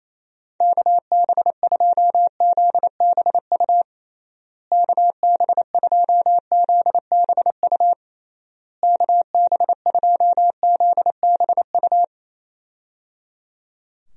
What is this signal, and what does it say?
Morse code, heard as a keyed tone, about 27 wpm, 700 Hz. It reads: KB2ZBU KB2ZBU KB2ZBU
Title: USA Call Signs - 6 Characters - Repeated 3x - 28wpm